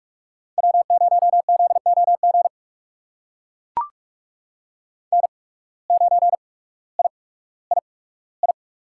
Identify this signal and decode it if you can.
Morse code, heard as a keyed tone, about 44 wpm, spaced as for 19 wpm, 700 Hz. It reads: W0ZOG N9III
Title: USA Call Signs - 5 Characters - 8x Character Spacing - 45wpm